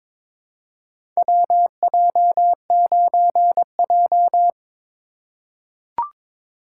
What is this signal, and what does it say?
Morse code, heard as a keyed tone, about 22 wpm, 700 Hz. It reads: WJ9J
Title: USA Call Signs - 4 Characters - 3x Character Spacing - 22wpm